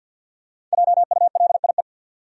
Morse code, heard as a keyed tone, about 53 wpm, 700 Hz. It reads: JULIE
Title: USA Common Names - 50wpm